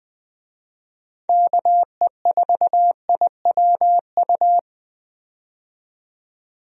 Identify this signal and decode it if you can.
Morse code, heard as a keyed tone, about 21 wpm, 700 Hz. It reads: KE4IWU
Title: USA Call Signs - 6 Characters - 20wpm